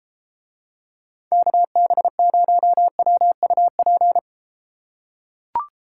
Morse code, heard as a keyed tone, about 33 wpm, 700 Hz. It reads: KB0WUP